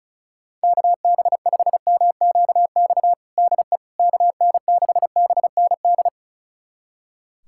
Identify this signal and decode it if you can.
Morse code, heard as a keyed tone, about 34 wpm, 700 Hz. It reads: KB5MQX DE KN6BND
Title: USA Callsign DE Callsign 35wpm